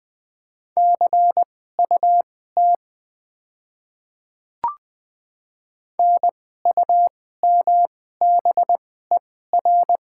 Morse code, heard as a keyed tone, about 20 wpm, spaced as for 14 wpm, 700 Hz. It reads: CUT NUMBER
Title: 2x Character Spacing - Top 500 Words - 20wpm